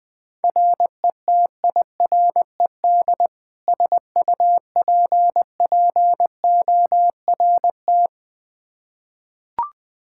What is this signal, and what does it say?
Morse code, heard as a keyed tone, about 21 wpm, 700 Hz. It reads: RETIRED SUPPORT
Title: K1USN SST QSO Practice 20wpm